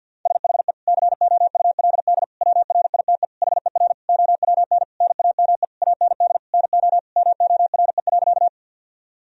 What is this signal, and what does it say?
Morse code, heard as a keyed tone, about 52 wpm, 700 Hz. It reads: SHE COULD WRITE HER OWN NAME AND NO MORE.
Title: Sentences from Top 1000 Words - Review Part 1 - 50wpm